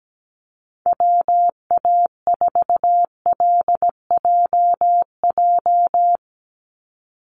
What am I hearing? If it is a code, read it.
Morse code, heard as a keyed tone, about 17 wpm, 700 Hz. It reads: WA4LJJ
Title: Call Signs Repeated 3x - 17wpm